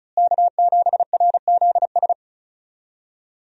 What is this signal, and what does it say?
Morse code, heard as a keyed tone, about 34 wpm, 700 Hz. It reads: K7RZS
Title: USA Call Signs - 5 Characters - Repeated 3x - 35wpm